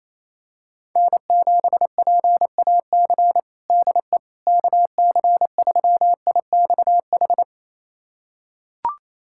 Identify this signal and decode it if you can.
Morse code, heard as a keyed tone, about 28 wpm, 700 Hz. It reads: N7PAC DE KC3IXH
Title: USA Callsign DE Callsign 28wpm